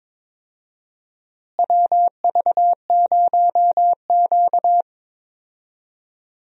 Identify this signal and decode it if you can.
Morse code, heard as a keyed tone, about 22 wpm, 700 Hz. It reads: WV0Q
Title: USA Call Signs - 4 Characters - Repeated 3x - 22wpm